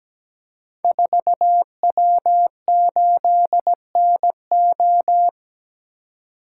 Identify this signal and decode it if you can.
Morse code, heard as a keyed tone, about 17 wpm, 700 Hz. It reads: 4W8NO
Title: Code Groups - 5 Letter:Numbers - 17wpm